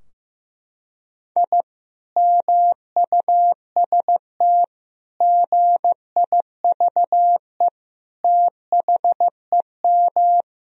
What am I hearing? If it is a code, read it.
Morse code, heard as a keyed tone, about 15 wpm, 700 Hz. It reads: I MUST GIVE THEM
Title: Sentences from Top 1000 Words - Review Part 2 - 15wpm